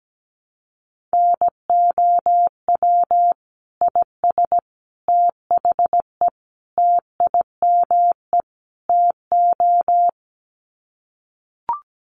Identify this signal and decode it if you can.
Morse code, heard as a keyed tone, about 17 wpm, 700 Hz. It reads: NOW IS THE TIME TO
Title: Sets of 5 Words Encore2 17wpm